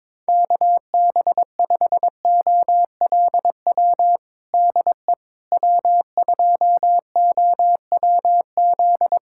Morse code, heard as a keyed tone, about 22 wpm, 700 Hz. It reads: KB5OLW DE W2OWZ
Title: USA Callsign DE Callsign 22wpm